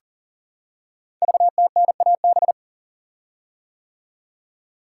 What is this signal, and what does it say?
Morse code, heard as a keyed tone, about 38 wpm, 690 Hz. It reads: VTNAB
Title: Code Groups - 5 Letters - 40wpm